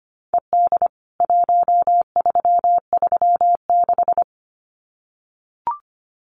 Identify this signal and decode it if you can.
Morse code, heard as a keyed tone, about 25 wpm, 700 Hz. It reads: ED 1336
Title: CWT Exchange 25wpm